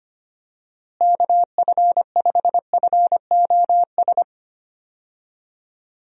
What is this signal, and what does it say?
Morse code, heard as a keyed tone, about 25 wpm, 700 Hz. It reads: KF5FOS
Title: USA Call Signs - 6 Characters - Repeated 3x - 25wpm